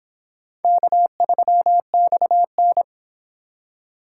Morse code, heard as a keyed tone, about 26 wpm, 700 Hz. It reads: K3XN